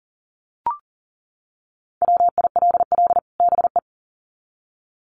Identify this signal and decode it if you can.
Morse code, heard as a keyed tone, about 40 wpm, 700 Hz. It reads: WILL BE
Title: Sets of 2 Words 40wpm